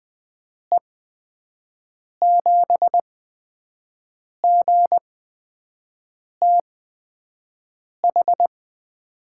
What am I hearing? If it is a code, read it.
Morse code, heard as a keyed tone, about 20 wpm, spaced as for 5 wpm, 700 Hz. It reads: E7GTH
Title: USA Call Signs - 6 Characters - 8x Character Spacing - 20wpm